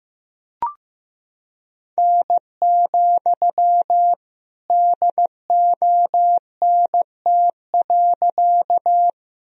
Morse code, heard as a keyed tone, about 15 wpm, 700 Hz. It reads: N, DONT.
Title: Sentences from Top 300 Words and Abbreviations 15wpm